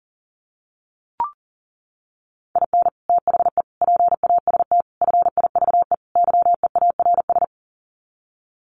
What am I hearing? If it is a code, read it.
Morse code, heard as a keyed tone, about 40 wpm, 700 Hz. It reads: IN THE PAST FIVE YEARS
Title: Sets of 5 Words Encore 40wpm